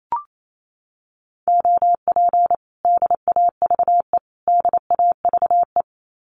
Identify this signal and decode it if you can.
Morse code, heard as a keyed tone, about 28 wpm, 700 Hz. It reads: OP DAVE DAVE
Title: Common QSO Phrases Encore 28wpm